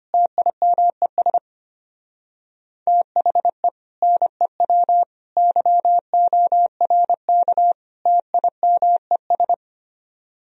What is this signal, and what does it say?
Morse code, heard as a keyed tone, about 30 wpm, 700 Hz. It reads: TIMES THE NEW YORK TIMES
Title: Speed-Racing - Sets of 4 Words 25wpm